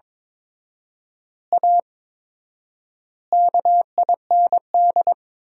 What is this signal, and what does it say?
Morse code, heard as a keyed tone, about 22 wpm, 700 Hz. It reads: A KIND
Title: 4x Word-Spacing - Sentences from Top 500 Words - 22wpm